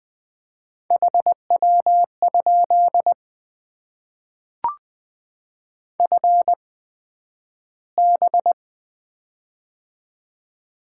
Morse code, heard as a keyed tone, about 20 wpm, spaced as for 11 wpm, 700 Hz. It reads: HW? FB
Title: Top 20 QSO Elements - 8x Character Spacing - 20wpm